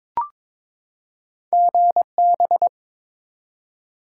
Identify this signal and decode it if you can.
Morse code, heard as a keyed tone, about 22 wpm, 700 Hz. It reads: GB
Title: Top 90 QSO Elements 22wpm